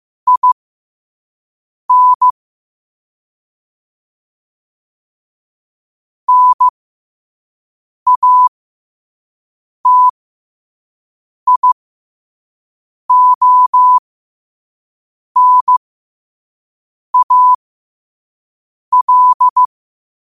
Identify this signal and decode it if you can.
Morse code, heard as a keyed tone, about 15 wpm, spaced as for 5 wpm, 1000 Hz. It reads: IN NATIONAL
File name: qotdfiles / 5wpm